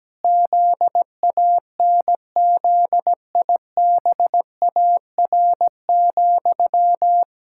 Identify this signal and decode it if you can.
Morse code, heard as a keyed tone, about 17 wpm, 700 Hz. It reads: ZANZIBAR,
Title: Pangram Challenge v2 - 17wpm